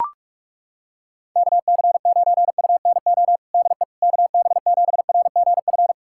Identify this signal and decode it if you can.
Morse code, heard as a keyed tone, about 42 wpm, 700 Hz. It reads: KC9UNO DE KB7RGF
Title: USA Callsign DE Callsign 45wpm